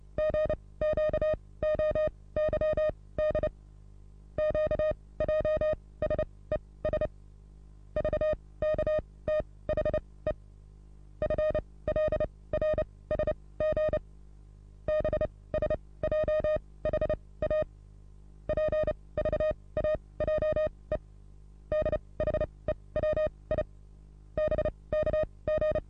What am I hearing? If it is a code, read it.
Morse code, heard as a keyed tone, about 30 wpm, spaced as for 20 wpm, 620 Hz. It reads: GQOYD QJSES VKTHE FLRSG BSJHA PVAJE DHEWI BKG
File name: repos / Morsekurs / DJ4UF Morsen27